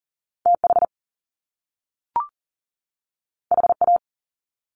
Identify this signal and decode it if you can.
Morse code, heard as a keyed tone, about 39 wpm, 700 Hz. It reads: TH HA
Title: Top Two Letter Combinations in Words 40wpm